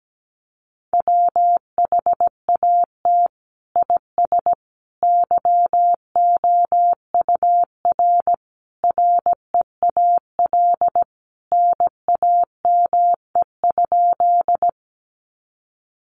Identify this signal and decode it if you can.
Morse code, heard as a keyed tone, about 17 wpm, 700 Hz. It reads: WHAT IS YOUR REAL NAME?
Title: Sentences from Top 300 Words 17wpm